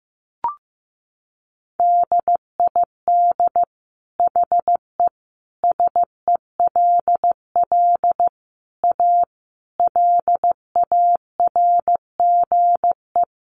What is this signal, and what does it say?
Morse code, heard as a keyed tone, about 15 wpm, 700 Hz. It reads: DID HE SELL A LARGE
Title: Sentences from Top 1300 Words 15wpm